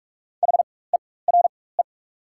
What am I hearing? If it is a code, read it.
Morse code, heard as a keyed tone, about 44 wpm, spaced as for 21 wpm, 690 Hz. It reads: HERE
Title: 4x Character Spacing - Top 200 Words - 45wpm